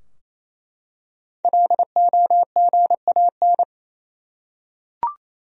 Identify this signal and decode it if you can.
Morse code, heard as a keyed tone, about 28 wpm, 700 Hz. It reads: LOGAN
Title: USA Common Names - Max 5 Letters - 28wpm